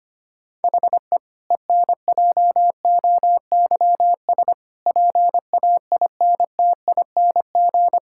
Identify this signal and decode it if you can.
Morse code, heard as a keyed tone, about 25 wpm, 700 Hz. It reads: HE ENJOYS PAINTING
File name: practice / 25wpm / Difficult Sentences with Hints - 25wpm